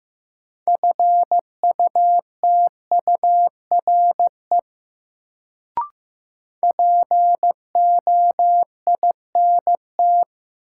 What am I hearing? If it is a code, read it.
Morse code, heard as a keyed tone, about 15 wpm, 700 Hz. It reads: FUTURE POINT
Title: Sets of 3 Words - New Word Drill - 15wpm